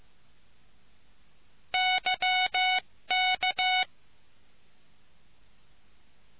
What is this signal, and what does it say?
Morse code, heard as a keyed tone, about 15 wpm, 740 Hz. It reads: YK